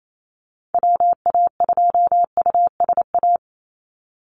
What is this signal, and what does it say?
Morse code, heard as a keyed tone, about 28 wpm, 700 Hz. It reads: WA2USA
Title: NA SOTA Chaser Callsigns 28wpm